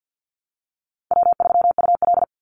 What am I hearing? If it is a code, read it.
Morse code, heard as a keyed tone, about 50 wpm, 700 Hz. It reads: W3UL
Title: Easy Call Signs Repeated 3x - 50wpm